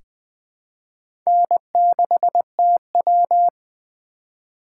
Morse code, heard as a keyed tone, about 20 wpm, 700 Hz. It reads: N6TW
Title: K1USN SST QSO Practice - 20wpm